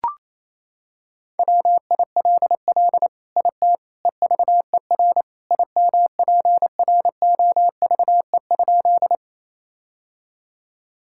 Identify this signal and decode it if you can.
Morse code, heard as a keyed tone, about 28 wpm, 700 Hz. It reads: WILL IT EVER IMPROVE?
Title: Sentences from Top 1000 Words - Review Part 5 - 28wpm